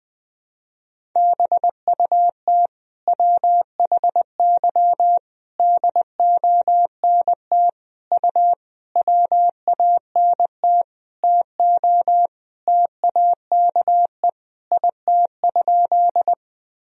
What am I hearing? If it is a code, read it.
Morse code, heard as a keyed tone, about 20 wpm, 700 Hz. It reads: BUT WHY DONT U WANT TO TAKE IT?
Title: Sentences from Top 300 Words and Abbreviations 20wpm